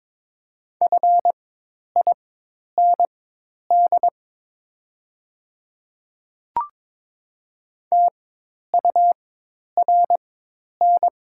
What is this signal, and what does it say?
Morse code, heard as a keyed tone, about 22 wpm, spaced as for 10 wpm, 690 Hz. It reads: FIND TURN
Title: 4x Character Spacing - Top 300 Words - 22wpm